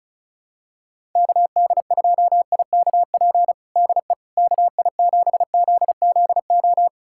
Morse code, heard as a keyed tone, about 34 wpm, 700 Hz. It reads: KD2IKP DE KI7ZZO
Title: USA Callsign DE Callsign - 6 Characters - 35wpm